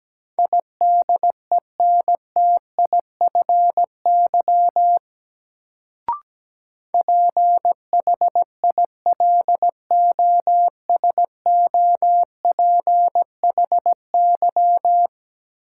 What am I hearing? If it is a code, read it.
Morse code, heard as a keyed tone, about 17 wpm, 700 Hz. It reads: IDENTIFY PHILOSOPHY
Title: Sentences from Top 900 Words - New Word Drill - 17wpm